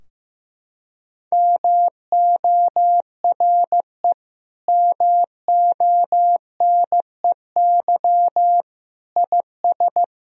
Morse code, heard as a keyed tone, about 15 wpm, 700 Hz. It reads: MORE MONEY IS
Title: Sentences from Top 1000 Words - Review Part 3 - 15wpm